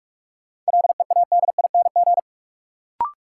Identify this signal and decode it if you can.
Morse code, heard as a keyed tone, about 45 wpm, 700 Hz. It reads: LEADING
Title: Sentences from Top 1100 Words - New Word Drill - 45wpm